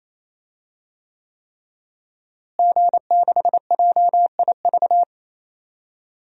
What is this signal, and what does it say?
Morse code, heard as a keyed tone, about 28 wpm, 700 Hz. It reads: G6JIV
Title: DXCC-100 Call Signs - 5 Characters - Repeated 3x - 28wpm